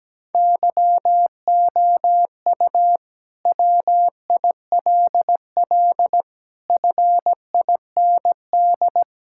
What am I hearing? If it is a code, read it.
Morse code, heard as a keyed tone, about 17 wpm, 700 Hz. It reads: YOU WILL FIND
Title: Sentences from Top 1000 Words - Review Part 5 - 17wpm